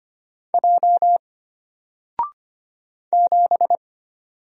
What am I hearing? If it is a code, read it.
Morse code, heard as a keyed tone, about 25 wpm, 700 Hz. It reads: J 7
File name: Single-Letter-Number-25wpm